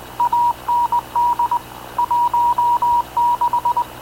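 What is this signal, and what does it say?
Morse code, heard as a keyed tone, about 19 wpm, 960 Hz. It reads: AND 16
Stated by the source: Havana Radio / CLA sending Spanish press (PX) in Morse, recorded by Radio Officer Paul Dumesnil onboard CNAV QUEST/CZDO.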